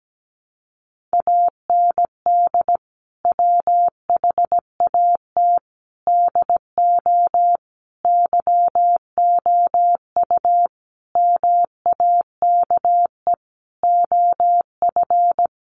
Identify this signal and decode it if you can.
Morse code, heard as a keyed tone, about 17 wpm, 700 Hz. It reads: AND WHAT DO YOU MAKE OF
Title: Sentences from Top 100 Words 17wpm